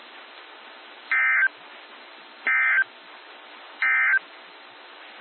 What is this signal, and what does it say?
Emergency Alert System (SAME) tones, heard as harsh buzzing bursts. It is heard from 1.0 to 4.3 s.
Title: NOAA Weather Radio - WXJ73